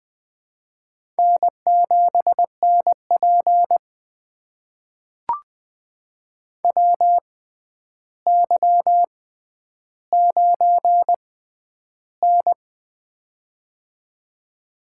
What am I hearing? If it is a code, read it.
Morse code, heard as a keyed tone, about 20 wpm, spaced as for 10 wpm, 700 Hz. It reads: N7NP WY9N
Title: USA Call Signs - 4 Characters - 6x Character Spacing - 20wpm